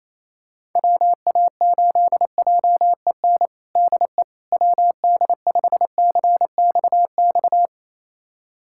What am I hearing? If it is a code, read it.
Morse code, heard as a keyed tone, about 28 wpm, 700 Hz. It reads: WA8JEN DE WD5CXX
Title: USA Callsign DE Callsign 28wpm